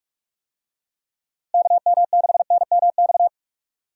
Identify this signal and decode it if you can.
Morse code, heard as a keyed tone, about 42 wpm, 700 Hz. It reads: KM6NMX
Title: USA Call Signs - 6 Characters - Repeated 2x - 45wpm